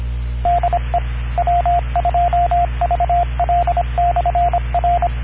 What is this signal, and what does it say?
Morse code, heard as a keyed tone, about 25 wpm, 700 Hz, with radio noise. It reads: DE W2VL/R